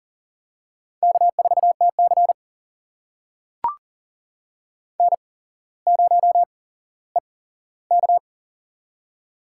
Morse code, heard as a keyed tone, about 42 wpm, spaced as for 17 wpm, 700 Hz. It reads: K4TC N0EK
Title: USA Call Signs - 4 Characters - 8x Character Spacing - 40wpm